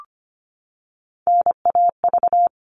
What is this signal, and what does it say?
Morse code, heard as a keyed tone, about 25 wpm, 700 Hz. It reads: NAV
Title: All Cut Numbers - 3-Digit 25wpm